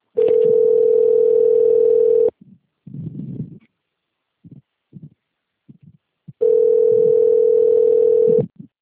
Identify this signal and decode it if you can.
Ringback tone (North America), 440 + 480 Hz.